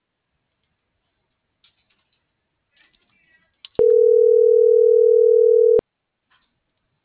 Ringback tone (North America), 440 + 480 Hz.